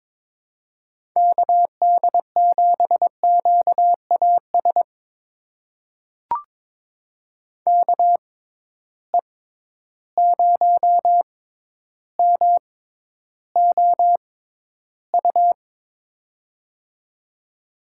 Morse code, heard as a keyed tone, about 22 wpm, spaced as for 11 wpm, 700 Hz. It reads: KD7QAS KE0MOU
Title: USA Call Signs - 6 Characters - 6x Character Spacing - 22wpm